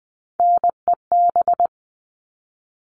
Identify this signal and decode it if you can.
Morse code, heard as a keyed tone, about 20 wpm, 700 Hz. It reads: NEB
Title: All Cut Numbers - 3-Digit 20wpm